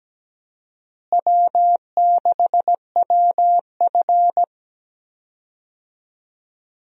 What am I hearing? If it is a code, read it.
Morse code, heard as a keyed tone, about 17 wpm, 700 Hz. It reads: W6WF
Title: USA Call Signs - 4 Characters - Repeated 2x - 17wpm